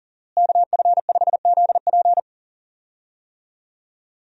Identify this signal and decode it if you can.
Morse code, heard as a keyed tone, about 40 wpm, 700 Hz. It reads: KF5ZP
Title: USA Call Signs - 5 Characters - Repeated 3x - 40wpm